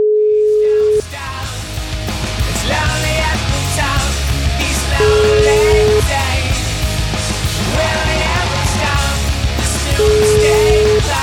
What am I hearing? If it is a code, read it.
Ringback tone (Europe, 425 Hz), with noise.